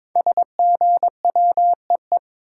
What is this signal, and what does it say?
Morse code, heard as a keyed tone, about 22 wpm, 700 Hz. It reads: SGWEE
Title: Code Groups - 5 Letters - 22wpm